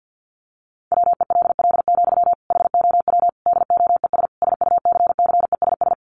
Morse code, heard as a keyed tone, about 50 wpm, 700 Hz. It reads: WELL, HOW DOES SUCCESS
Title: Sentences from Top 1100 Words 50wpm